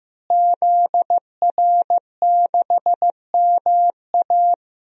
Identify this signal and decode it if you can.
Morse code, heard as a keyed tone, about 14 wpm, 700 Hz. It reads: ZR6MA
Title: DXCC-100 Call Signs - 5 Characters - Repeated 3x - 15wpm